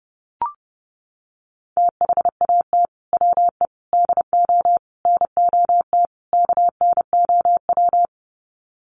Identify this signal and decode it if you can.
Morse code, heard as a keyed tone, about 30 wpm, 700 Hz. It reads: THAT WE DO NOT KNOW
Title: Sets of 5 Words Encore2 30wpm